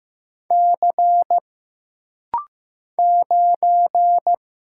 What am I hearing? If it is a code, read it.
Morse code, heard as a keyed tone, about 15 wpm, 700 Hz. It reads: C 9
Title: Single Letter-Number Alt - Rapid-Fire 15wpm